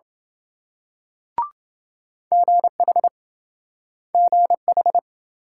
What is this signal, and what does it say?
Morse code, heard as a keyed tone, about 27 wpm, 700 Hz. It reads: GH GH